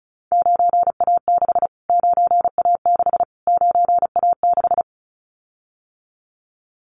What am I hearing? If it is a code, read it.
Morse code, heard as a keyed tone, about 34 wpm, 700 Hz. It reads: 9A6 9A6 9A6
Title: Callsign Prefixes for Top 25 DXCC - Repeated 3x - 35wpm